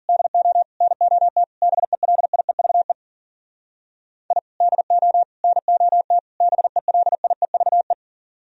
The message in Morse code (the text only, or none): DO NOT BELIEVE I DO NOT BELIEVE